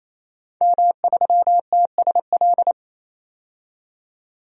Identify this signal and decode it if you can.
Morse code, heard as a keyed tone, about 27 wpm, 700 Hz. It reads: M3TSL
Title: DXCC-100 Call Signs - 5 Characters - 28wpm